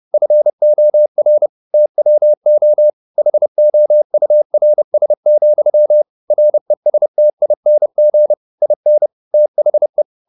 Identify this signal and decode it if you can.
Morse code, heard as a keyed tone, about 30 wpm, 600 Hz. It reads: FOR TWO HOURS, RESTING IN THE